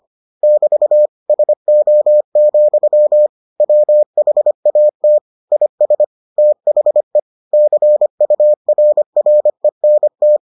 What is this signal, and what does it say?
Morse code, heard as a keyed tone, about 25 wpm, 600 Hz. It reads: <BT> SO, WHAT IS THE CURRENT